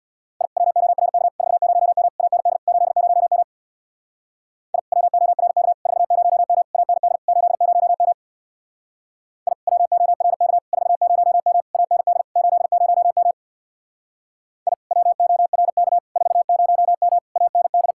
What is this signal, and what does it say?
Morse code, heard as a keyed tone, about 62 wpm, 700 Hz. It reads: I WORK 40M AND 80M I WORK 40M AND 80M I WORK 40M AND 80M I WORK 40M AND
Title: Speed-Racing Common QSO Phrases - 50wpm